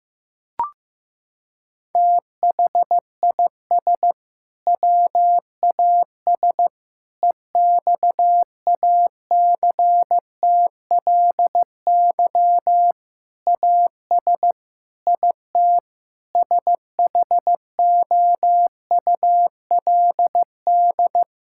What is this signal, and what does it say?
Morse code, heard as a keyed tone, about 15 wpm, 700 Hz. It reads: THIS WAS EXACTLY AS IT SHOULD
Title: Sentences from Top 1000 Words - Review Part 6 - 15wpm